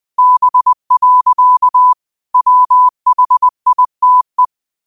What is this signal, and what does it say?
Morse code, heard as a keyed tone, about 20 wpm, 1000 Hz. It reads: B. WHITE
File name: qotdfiles / 20wpm